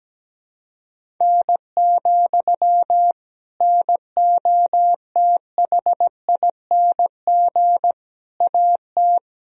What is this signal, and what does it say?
Morse code, heard as a keyed tone, about 17 wpm, 700 Hz. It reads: N, NOTHING AT
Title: Sentences from Top 500 Words and Abbreviations 17wpm